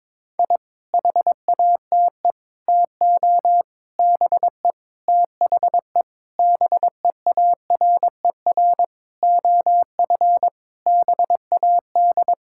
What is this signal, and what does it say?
Morse code, heard as a keyed tone, about 22 wpm, 700 Hz. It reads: I HATE TO BE THE BEARER OF BAD
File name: Sets of 9 Words - 22wpm